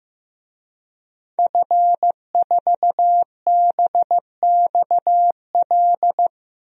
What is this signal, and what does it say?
Morse code, heard as a keyed tone, about 15 wpm, 700 Hz. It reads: F4BXL
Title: DXCC-100 Call Signs - 5 Characters - Repeated 3x - 15wpm